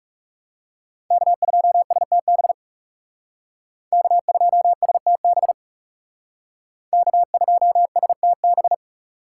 Morse code, heard as a keyed tone, about 35 wpm, 700 Hz. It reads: K2STB K2STB K2STB
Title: Speed Racing - USA 1x3 Call Signs 30wpm